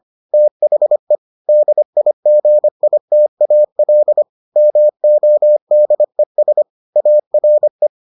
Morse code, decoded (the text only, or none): THE DIGITAL MODES ARE